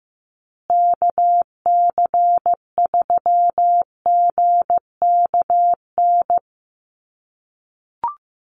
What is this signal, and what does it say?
Morse code, heard as a keyed tone, about 15 wpm, 700 Hz. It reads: KC3GKN
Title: USA Call Signs Repeated 2x 15wpm